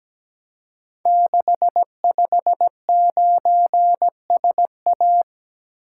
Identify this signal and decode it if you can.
Morse code, heard as a keyed tone, about 17 wpm, 700 Hz. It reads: 659SA
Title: Code Groups - 5 Letter:Numbers - 17wpm